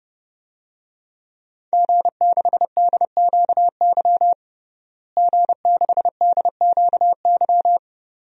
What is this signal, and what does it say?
Morse code, heard as a keyed tone, about 30 wpm, 700 Hz. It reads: G6DQY G6DQY
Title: DXCC-100 Call Signs - 5 Characters - Repeated 2x - 30wpm